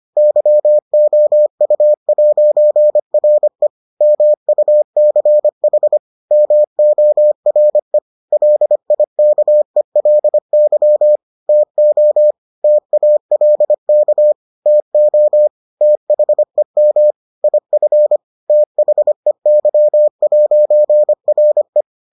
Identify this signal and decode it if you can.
Morse code, heard as a keyed tone, about 25 wpm, 600 Hz. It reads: YOU'RE MUCH MORE LIKELY TO TALK TO THEM IF THEY'RE